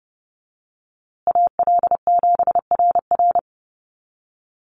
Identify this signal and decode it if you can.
Morse code, heard as a keyed tone, about 30 wpm, 700 Hz. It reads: AL7RR
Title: USA Call Signs Repeated 3x 30wpm